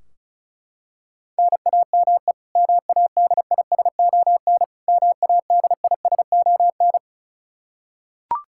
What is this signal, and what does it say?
Morse code, heard as a keyed tone, about 34 wpm, 700 Hz. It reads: NAME MADISON MADISON
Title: K1USN SST QSO Practice 35wpm